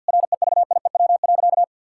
Morse code, heard as a keyed tone, about 50 wpm, 700 Hz. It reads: REVIEW.
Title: Sentences from Top 1000 Words - Review Part 6 - 50wpm